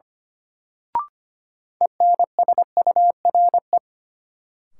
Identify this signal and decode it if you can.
Morse code, heard as a keyed tone, about 25 wpm, 700 Hz. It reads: ENSURE
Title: Sentences from Top 1100 Words - New Word Drill Alt - 25wpm